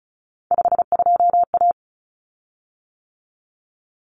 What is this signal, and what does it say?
Morse code, heard as a keyed tone, about 34 wpm, 700 Hz. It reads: 52A
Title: Common Cut Numbers - 3-Digit 35wpm